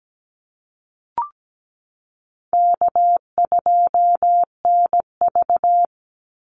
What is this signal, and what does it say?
Morse code, heard as a keyed tone, about 17 wpm, 700 Hz. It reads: K2NV
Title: Easy Call Signs - 17wpm